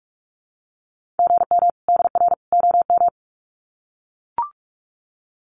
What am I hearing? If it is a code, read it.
Morse code, heard as a keyed tone, about 44 wpm, 700 Hz. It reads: GM DR OM
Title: Common QSO Phrases 45wpm